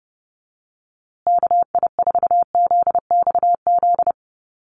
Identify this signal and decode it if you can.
Morse code, heard as a keyed tone, about 30 wpm, 700 Hz. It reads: KI4ZXZ
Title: USA Call Signs Repeated 2x 30wpm